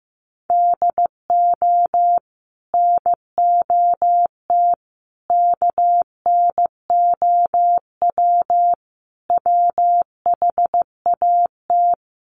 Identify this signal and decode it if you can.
Morse code, heard as a keyed tone, about 15 wpm, 700 Hz. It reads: DO NOT KNOW WHAT
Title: Sets of 5 Words 15wpm